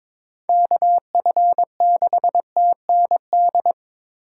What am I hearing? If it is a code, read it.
Morse code, heard as a keyed tone, about 22 wpm, 700 Hz. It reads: KF6TND
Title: USA Call Signs - 6 Characters - 4x Character Spacing - 22wpm